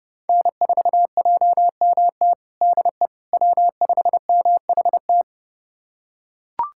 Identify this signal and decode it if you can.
Morse code, heard as a keyed tone, about 32 wpm, 700 Hz. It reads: N4JMT DE W5MHT
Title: USA Callsign DE Callsign - 5 Characters - 30wpm